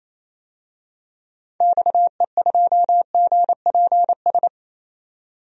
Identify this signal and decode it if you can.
Morse code, heard as a keyed tone, about 28 wpm, 700 Hz. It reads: XE2GPS
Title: DXCC-100 Call Signs - 6 Characters - 28wpm